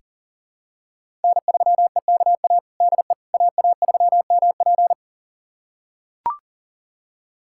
Morse code, heard as a keyed tone, about 40 wpm, 700 Hz. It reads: N3EKA DE AA3MP
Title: USA Callsign DE Callsign - 5 Characters - 40wpm